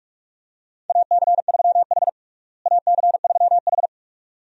Morse code, heard as a keyed tone, about 46 wpm, 700 Hz. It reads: AC3H AC3H
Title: USA Call Signs - 4 Characters - Repeated 2x - 45wpm